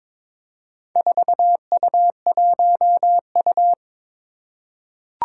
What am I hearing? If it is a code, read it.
Morse code, heard as a keyed tone, about 22 wpm, 700 Hz. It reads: 4U1U
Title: Four Challenging Letter-Number Combinations 22wpm